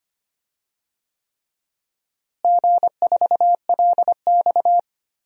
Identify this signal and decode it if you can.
Morse code, heard as a keyed tone, about 25 wpm, 700 Hz. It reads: G4LX
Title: DXCC-100 Call Signs - 4 Characters - Repeated 3x - 25wpm